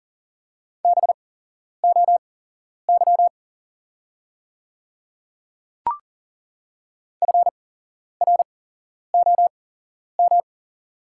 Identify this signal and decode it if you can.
Morse code, heard as a keyed tone, about 40 wpm, spaced as for 11 wpm, 700 Hz. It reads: BOY FROM
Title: 8x Character Spacing - Top 200 Words - 40wpm